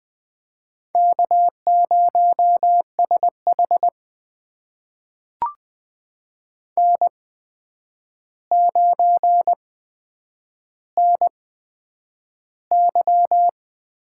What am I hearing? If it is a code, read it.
Morse code, heard as a keyed tone, about 20 wpm, spaced as for 9 wpm, 700 Hz. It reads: K0SH N9NY
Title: USA Call Signs - 4 Characters - 8x Character Spacing - 20wpm